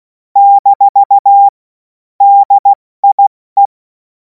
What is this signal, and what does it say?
Morse code, heard as a keyed tone, about 16 wpm, spaced as for 14 wpm, 800 Hz. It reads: - DIE
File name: Morsekurs / Witze / Witz 11. August 2026 70 F80